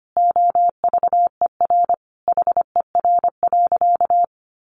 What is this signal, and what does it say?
Morse code, heard as a keyed tone, about 25 wpm, 700 Hz. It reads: OVER HER.
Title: Sentences from Top 1200 Words 25wpm